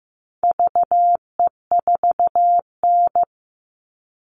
Morse code, heard as a keyed tone, about 15 wpm, 700 Hz. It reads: VE4N
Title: All Cut Numbers - 4-Digit 15wpm